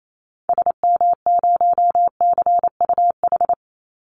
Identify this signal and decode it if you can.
Morse code, heard as a keyed tone, about 28 wpm, 700 Hz. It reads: SM0CUH
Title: Call Signs Repeated 2x - 28wpm